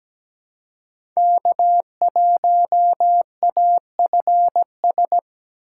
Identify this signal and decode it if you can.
Morse code, heard as a keyed tone, about 17 wpm, 700 Hz. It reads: K1AFS
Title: USA Call Signs - 5 Characters - 4x Character Spacing - 17wpm